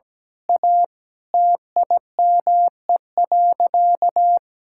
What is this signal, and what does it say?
Morse code, heard as a keyed tone, about 17 wpm, 700 Hz. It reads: A TIME.
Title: Sentences from Top 1000 Words - Review Part 6 - 17wpm